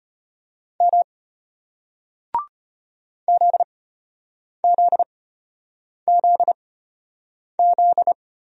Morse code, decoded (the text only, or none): M ZZZZ